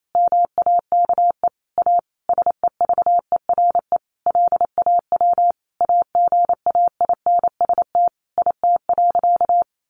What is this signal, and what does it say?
Morse code, heard as a keyed tone, about 28 wpm, 700 Hz. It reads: MAKE A SEVERE LAW AGAINST IT.
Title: Sentences from Top 1400 Words 28wpm